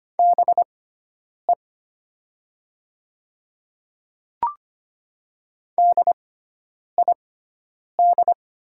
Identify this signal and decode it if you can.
Morse code, heard as a keyed tone, about 25 wpm, spaced as for 9 wpm, 700 Hz. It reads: BE DID
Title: 6x Character Spacing - Top 100 Words - 25wpm